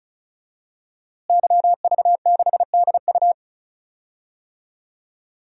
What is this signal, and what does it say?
Morse code, heard as a keyed tone, about 36 wpm, 700 Hz. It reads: YV6DU
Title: DXCC-100 Call Signs - 5 Characters - Repeated 3x - 35wpm